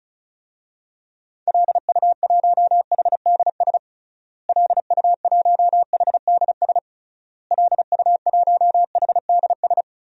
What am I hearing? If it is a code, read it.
Morse code, heard as a keyed tone, about 34 wpm, 700 Hz. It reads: LU1HDS LU1HDS LU1HDS
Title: DXCC-100 Call Signs - 6 Characters - Repeated 3x - 35wpm